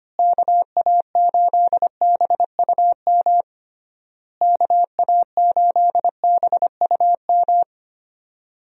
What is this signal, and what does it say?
Morse code, heard as a keyed tone, about 25 wpm, 700 Hz. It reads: KA8BUM KA8BUM
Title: USA Call Signs - 6 Characters - Repeated 3x - 25wpm